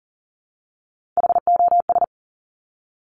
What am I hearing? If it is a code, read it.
Morse code, heard as a keyed tone, about 39 wpm, 700 Hz. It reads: HOS
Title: Top Three Letter Combinations in Words 40wpm